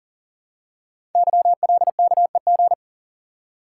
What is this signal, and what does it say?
Morse code, heard as a keyed tone, about 40 wpm, 700 Hz. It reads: YLKEG
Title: Code Groups - 5 Letters - 40wpm